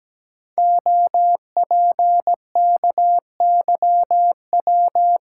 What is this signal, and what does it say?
Morse code, heard as a keyed tone, about 17 wpm, 700 Hz. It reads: OPKYW
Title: Code Groups - 5 Letter:Numbers - 17wpm